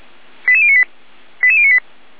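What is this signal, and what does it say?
Five-tone selective call (ZVEI), heard as quick runs of five tones, tones of about 70 ms. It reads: 88098 88098
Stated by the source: ZVEI five-tone selective call received on an SDR.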